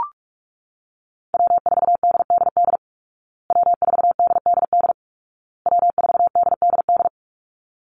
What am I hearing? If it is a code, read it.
Morse code, heard as a keyed tone, about 44 wpm, 700 Hz. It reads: W4DDD W4DDD W4DDD
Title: Call Signs Repeated 2x - 45wpm